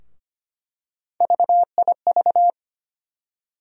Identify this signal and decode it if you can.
Morse code, heard as a keyed tone, about 25 wpm, 700 Hz. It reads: VIV
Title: USA Call Sign Postfixes - 3 Letters 25wpm